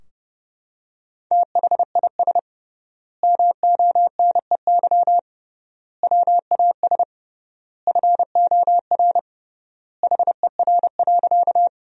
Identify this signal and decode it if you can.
Morse code, heard as a keyed tone, about 29 wpm, 700 Hz. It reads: THIS MONEY WAS FOR HER.
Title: 3x Word-Spacing - Sentences from Top 400 Words - 30wpm